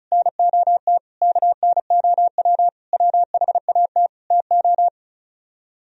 Morse code, heard as a keyed tone, about 43 wpm, 700 Hz. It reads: NOT KNOW WHAT TO
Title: Speed-Racing - Sets of 5 Words 35wpm